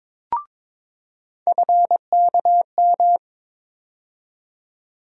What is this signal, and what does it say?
Morse code, heard as a keyed tone, about 22 wpm, 700 Hz. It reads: FKM
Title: USA Call Sign Postfixes - 3 Letters 22wpm